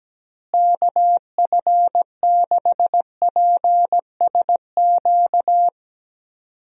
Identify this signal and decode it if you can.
Morse code, heard as a keyed tone, about 17 wpm, 700 Hz. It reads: KF6PSQ
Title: USA Call Signs - 6 Characters - Repeated 2x - 17wpm